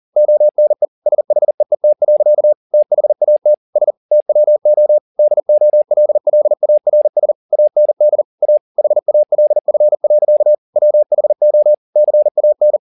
Morse code, heard as a keyed tone, about 40 wpm, 600 Hz. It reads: ONE SHEET. THAT S TWO DOLLARS AND A HALF. WHO CAN